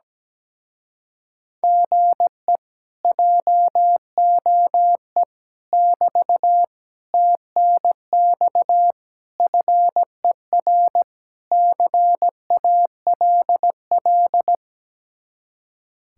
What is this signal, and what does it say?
Morse code, heard as a keyed tone, about 17 wpm, 700 Hz. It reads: GE JOE <BT> TNX FER CALL
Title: Speed-Racing Common QSO Phrases Encore - 17wpm